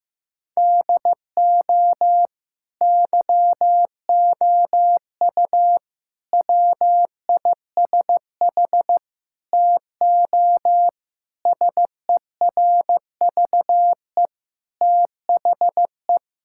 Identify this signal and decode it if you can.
Morse code, heard as a keyed tone, about 15 wpm, 700 Hz. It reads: DO YOU WISH TO SERVE THE
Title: Sentences from Top 1600 Words 15wpm